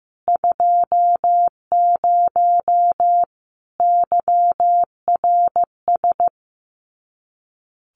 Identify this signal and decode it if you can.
Morse code, heard as a keyed tone, about 15 wpm, 700 Hz. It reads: 20 YRS
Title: Common QSO Phrases 15wpm